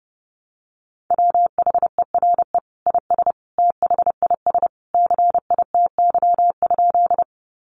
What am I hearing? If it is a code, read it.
Morse code, heard as a keyed tone, about 29 wpm, 700 Hz. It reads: WHERE IS THIS CITY?
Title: Sentences from Top 300 Words 30wpm